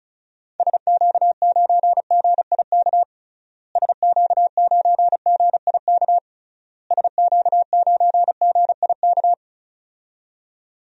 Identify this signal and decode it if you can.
Morse code, heard as a keyed tone, about 36 wpm, 700 Hz. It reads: SQ9GIK SQ9GIK SQ9GIK
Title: DXCC-100 Call Signs - 6 Characters - Repeated 3x - 35wpm